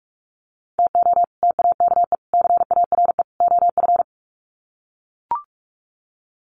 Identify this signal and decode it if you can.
Morse code, heard as a keyed tone, about 44 wpm, 700 Hz. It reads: TO TAKE CARE OF
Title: Sets of 4 Words 45wpm